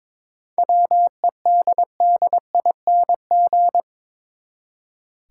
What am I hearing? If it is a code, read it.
Morse code, heard as a keyed tone, about 22 wpm, 700 Hz. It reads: WEDDING
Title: Seven Letter Words - 22wpm